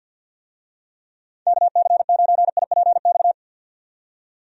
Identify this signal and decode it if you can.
Morse code, heard as a keyed tone, about 51 wpm, 700 Hz. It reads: KC9IPX
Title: USA Call Signs - 6 Characters - 50wpm